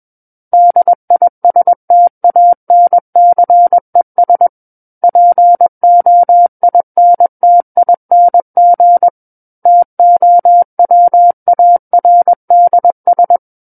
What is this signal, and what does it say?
Morse code, heard as a keyed tone, about 21 wpm, 700 Hz. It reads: DISTANCES POINTING TOWARDS
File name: Book One / 12-Chapter 12 - 21 WPM